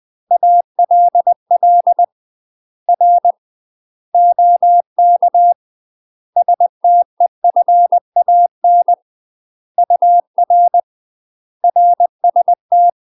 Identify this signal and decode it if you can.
Morse code, heard as a keyed tone, about 20 wpm, 700 Hz. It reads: ALL R OK STEFAN UR RST
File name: UK MASTER QSOs TRAINING / 20WPM / UK MASTER QSOs 0015